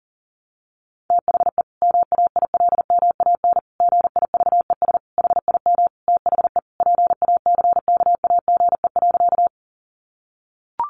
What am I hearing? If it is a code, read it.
Morse code, heard as a keyed tone, about 40 wpm, 700 Hz. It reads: THE MAILMAN GIVES HIM THE PACKAGE.
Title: Easy Phrases 40wpm